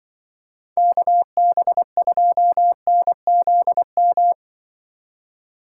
Morse code, heard as a keyed tone, about 24 wpm, 700 Hz. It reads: KB2NZM
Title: Speed Racing - USA 2x3 Call Signs 20wpm